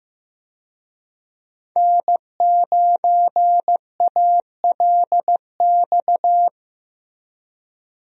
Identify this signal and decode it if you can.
Morse code, heard as a keyed tone, about 16 wpm, 700 Hz. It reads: N9ALX
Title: USA Call Signs - 5 Characters - 15wpm